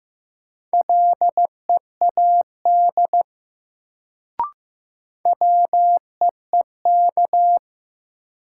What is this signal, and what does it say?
Morse code, heard as a keyed tone, about 15 wpm, 700 Hz. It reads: LEAD WEEK
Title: Sentences from Top 500 Words - New Word Drill - 15wpm